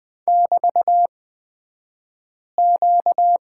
Morse code, heard as a keyed tone, about 20 wpm, spaced as for 5 wpm, 690 Hz. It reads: <BT>Q
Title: 235 Lesson 38d - Letter q - Two Characters 20wpm